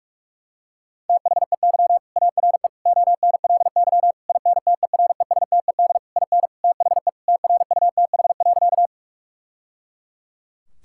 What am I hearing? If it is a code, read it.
Morse code, heard as a keyed tone, about 41 wpm, 700 Hz. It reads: THEY ARE ONLY INTERESTED IN THE TRUTH.
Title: Sentences from Top 1000 Words - Review Part 6 - 45wpm